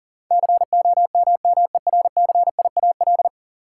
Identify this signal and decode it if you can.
Morse code, heard as a keyed tone, about 40 wpm, 700 Hz. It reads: COMMERCIAL
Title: Sentences from Top 1000 Words - Review Part 5 - New Word Drill - 40wpm